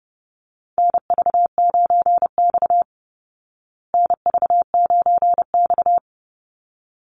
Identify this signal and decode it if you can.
Morse code, heard as a keyed tone, about 30 wpm, 700 Hz. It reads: NV9X NV9X
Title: Call Signs Repeated 3x - 30wpm